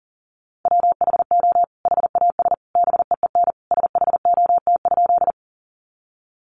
Morse code, heard as a keyed tone, about 40 wpm, 700 Hz. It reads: WHO HAS BEEN SHOT?
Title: Sentences from Top 1100 Words 40wpm